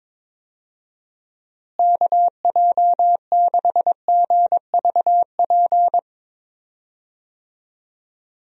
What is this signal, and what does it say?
Morse code, heard as a keyed tone, about 22 wpm, 700 Hz. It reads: KJ6GVP
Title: USA Call Signs - 6 Characters - Repeated 3x - 22wpm